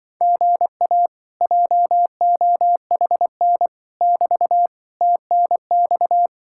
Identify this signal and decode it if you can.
Morse code, heard as a keyed tone, about 24 wpm, 700 Hz. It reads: GA JOHN <BT> TNX
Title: Speed-Racing Common QSO Phrases - 20wpm